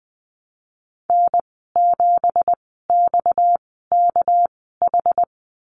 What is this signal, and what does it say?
Morse code, heard as a keyed tone, about 19 wpm, spaced as for 14 wpm, 700 Hz. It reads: N7XKH
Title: 231 Lesson 37g - Letter x - Call Signs 20wpm